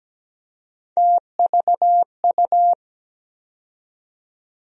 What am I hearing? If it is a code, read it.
Morse code, heard as a keyed tone, about 17 wpm, 700 Hz. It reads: TVU
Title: USA Call Sign Postfixes - 3 Letters 17wpm